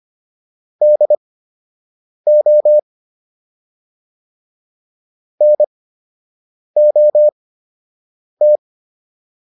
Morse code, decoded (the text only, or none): DO NOT